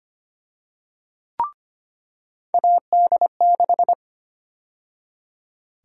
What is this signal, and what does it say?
Morse code, heard as a keyed tone, about 25 wpm, 700 Hz. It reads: AD6
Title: USA Call Sign Prefixes - 25wpm